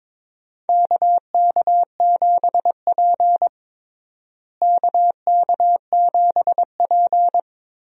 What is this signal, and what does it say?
Morse code, heard as a keyed tone, about 22 wpm, 700 Hz. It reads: KK7P KK7P
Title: USA Call Signs - 4 Characters - Repeated 2x - 22wpm